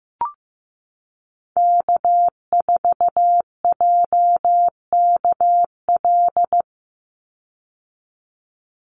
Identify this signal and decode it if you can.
Morse code, heard as a keyed tone, about 15 wpm, 700 Hz. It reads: K4JKL
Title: Call Signs Repeated 3x - 15wpm